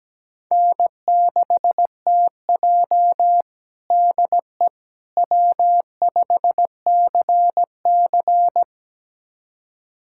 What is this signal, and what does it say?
Morse code, heard as a keyed tone, about 17 wpm, 700 Hz. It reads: N6TJ DE W5CC
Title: USA Callsign DE Callsign - 4 Characters - 17wpm